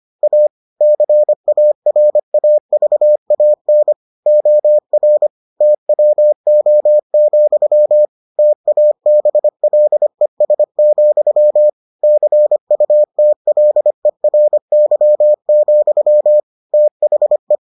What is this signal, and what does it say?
Morse code, heard as a keyed tone, about 25 wpm, 600 Hz. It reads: A CARAVAN OR TWO, TABLES, CUTLERY, THE